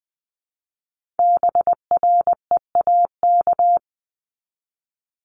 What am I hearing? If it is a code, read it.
Morse code, heard as a keyed tone, about 20 wpm, 700 Hz. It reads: BREAK
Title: Multi-letter Phonemes Repeated 3x 20wpm